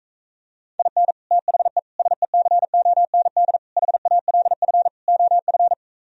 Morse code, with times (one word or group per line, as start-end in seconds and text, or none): 0.8-1.1 IN
1.3-1.8 THE
2.0-3.6 SECOND
3.8-4.9 HALF
5.1-5.7 OF